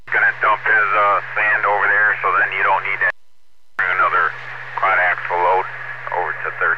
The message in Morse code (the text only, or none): none